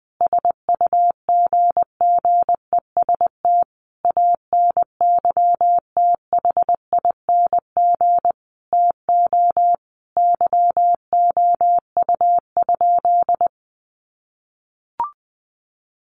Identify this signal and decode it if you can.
Morse code, heard as a keyed tone, about 20 wpm, 700 Hz. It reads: SUGGEST ANYTHING TO YOU?
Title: Sentences from Top 1300 Words 20wpm